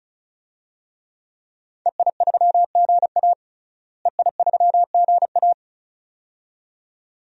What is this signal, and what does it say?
Morse code, heard as a keyed tone, about 36 wpm, 700 Hz. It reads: EI3GA EI3GA
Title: DXCC-100 Call Signs - 5 Characters - Repeated 2x - 35wpm